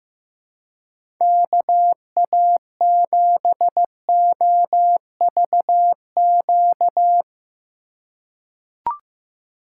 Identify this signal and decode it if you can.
Morse code, heard as a keyed tone, about 15 wpm, 700 Hz. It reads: KA7OVQ
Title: USA Call Signs - 6 Characters - 4x Character Spacing - 15wpm